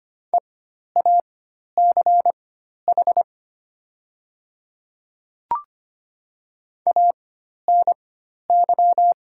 Morse code, heard as a keyed tone, about 25 wpm, spaced as for 12 wpm, 700 Hz. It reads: EACH ANY
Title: 4x Character Spacing - Top 200 Words - 25wpm